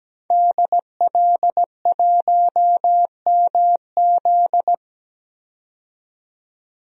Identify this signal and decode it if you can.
Morse code, heard as a keyed tone, about 17 wpm, 700 Hz. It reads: DL1MZ
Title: DXCC-100 Call Signs - 5 Characters - 17wpm